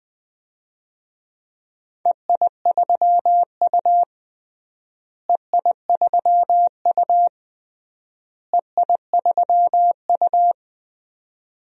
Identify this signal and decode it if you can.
Morse code, heard as a keyed tone, about 20 wpm, 700 Hz. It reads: EI3U EI3U EI3U
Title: DXCC-100 Call Signs - 4 Characters - Repeated 3x - 20wpm